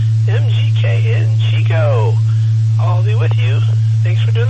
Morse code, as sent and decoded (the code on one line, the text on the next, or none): none